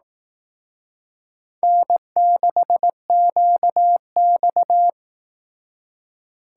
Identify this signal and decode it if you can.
Morse code, heard as a keyed tone, about 18 wpm, 700 Hz. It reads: N6QX
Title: Speed Racing - USA 1x2 Call Signs 15wpm